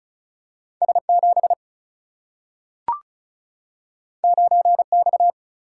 Morse code, heard as a keyed tone, about 36 wpm, 700 Hz. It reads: S7 9X
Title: Single-Letter Single-Number 35wpm